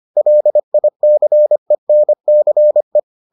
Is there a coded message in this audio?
Yes: Morse code from 0.2 to 3.0 s, heard as a keyed tone, about 25 wpm, 600 Hz. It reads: LICENCE